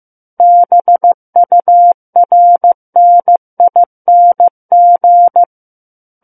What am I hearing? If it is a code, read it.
Morse code, heard as a keyed tone, about 15 wpm, 700 Hz. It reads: BURNING
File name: Book One / 06-Chapter 6 - 15 WPM